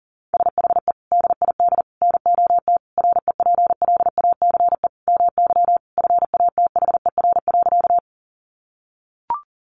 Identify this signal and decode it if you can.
Morse code, heard as a keyed tone, about 40 wpm, 700 Hz. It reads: SHE DID NOT REPLACE MY FATHER.
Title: Sentences from Top 1400 Words 40wpm